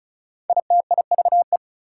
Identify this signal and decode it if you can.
Morse code, heard as a keyed tone, about 34 wpm, 700 Hz. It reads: ITIVE
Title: Prefixes and Suffixes 35wpm